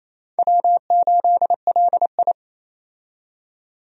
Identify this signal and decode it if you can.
Morse code, heard as a keyed tone, about 27 wpm, 700 Hz. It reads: W8LI
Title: USA Call Signs - 4 Characters - 28wpm